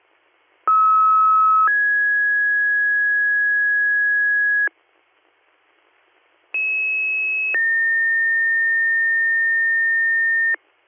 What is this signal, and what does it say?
Two-tone page, heard as a one-second tone, then a three-second tone, about 1290 then 1650 Hz, about 2600 then 1800 Hz. It is heard from 0.7 to 10.6 s.